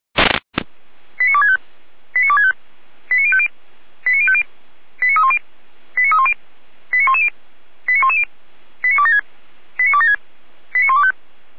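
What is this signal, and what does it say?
Five-tone selective call (ZVEI), heard as quick runs of five tones, tones of about 70 ms. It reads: 89265 89265 89050 89050 89310 89310 89110 89110 89276 89276 89125
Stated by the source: ZVEI five-tone call received by SDR, from feuerwehr-orlamuende (fire brigade).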